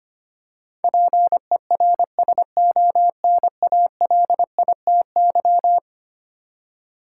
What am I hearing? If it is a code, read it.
Morse code, heard as a keyed tone, about 25 wpm, 700 Hz. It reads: PERSONALITY
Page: morse-code-ninja-practice-25wpm